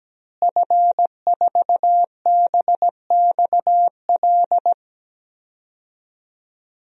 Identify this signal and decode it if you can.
Morse code, heard as a keyed tone, about 17 wpm, 700 Hz. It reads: F4BXL
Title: DXCC-100 Call Signs - 5 Characters - Repeated 3x - 17wpm